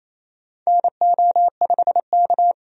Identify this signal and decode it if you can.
Morse code, heard as a keyed tone, about 27 wpm, 700 Hz. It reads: NO5K
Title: USA Call Signs - 4 Characters - 28wpm